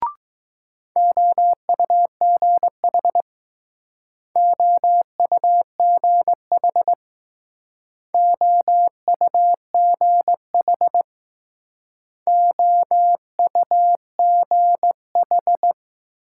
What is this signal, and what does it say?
Morse code, heard as a keyed tone, about 20 wpm, 700 Hz. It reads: OUGH OUGH OUGH OUGH